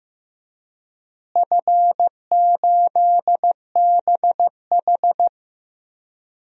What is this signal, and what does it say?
Morse code, heard as a keyed tone, about 15 wpm, 700 Hz. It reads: F8BH
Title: DXCC-100 Call Signs - 4 Characters - Repeated 3x - 15wpm